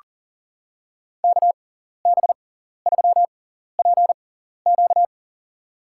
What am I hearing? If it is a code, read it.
Morse code, heard as a keyed tone, about 40 wpm, spaced as for 14 wpm, 700 Hz. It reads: KB3PQ
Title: USA Call Signs - 6x Character Spacing - 40wpm